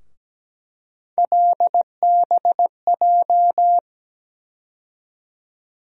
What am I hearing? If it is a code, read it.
Morse code, heard as a keyed tone, about 17 wpm, 700 Hz. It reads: LBJ
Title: USA Call Sign Postfixes - 3 Letters 17wpm